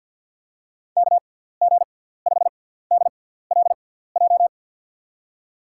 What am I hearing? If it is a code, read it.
Morse code, heard as a keyed tone, about 48 wpm, spaced as for 17 wpm, 700 Hz. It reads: KG5DLJ
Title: USA Call Signs - 6 Characters - 6x Character Spacing - 50wpm